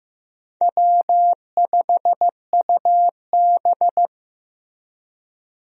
Morse code, heard as a keyed tone, about 15 wpm, 700 Hz. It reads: W5UB
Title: USA Call Signs - 4 Characters - Repeated 2x - 15wpm